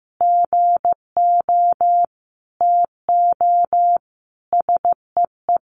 Morse code, heard as a keyed tone, about 15 wpm, 700 Hz. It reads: GO TO SEE
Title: Sentences from Top 1500 Words 15wpm